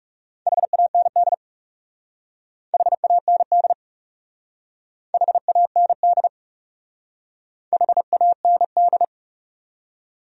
Morse code, heard as a keyed tone, about 36 wpm, 700 Hz. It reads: HAND HAND HAND HAND
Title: Speed-Racing Top 200 Words 30wpm